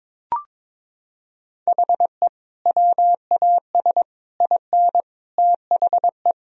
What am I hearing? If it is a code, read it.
Morse code, heard as a keyed tone, about 22 wpm, 700 Hz. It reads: HE WAS IN THE
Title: Sentences from Top 1000 Words - Review Part 2 - 22wpm